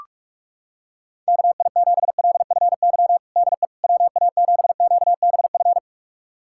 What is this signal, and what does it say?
Morse code, heard as a keyed tone, about 43 wpm, 700 Hz. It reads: KI7LFY DE WA7QBF
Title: USA Callsign DE Callsign 45wpm